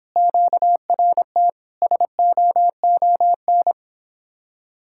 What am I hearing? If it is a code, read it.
Morse code, heard as a keyed tone, about 26 wpm, 700 Hz. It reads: QRT SOON